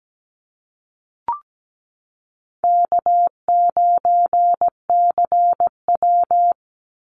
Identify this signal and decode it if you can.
Morse code, heard as a keyed tone, about 17 wpm, 700 Hz. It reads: K9CW
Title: Easy Call Signs - 17wpm